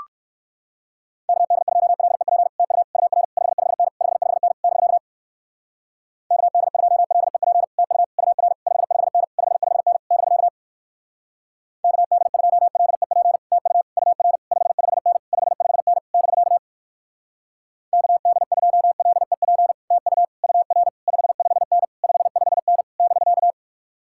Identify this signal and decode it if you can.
Morse code, heard as a keyed tone, about 64 wpm, 700 Hz. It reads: KD2LEP TU UR 55N 55N <BK> KD2LEP TU UR 55N 55N <BK> KD2LEP TU UR 55N 55N <BK> KD2LEP TU UR 55N 55N <BK>